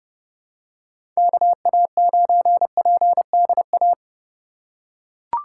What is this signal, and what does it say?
Morse code, heard as a keyed tone, about 31 wpm, 700 Hz. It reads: KA9PDA
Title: USA Call Signs - 6 Characters - 4x Character Spacing - 30wpm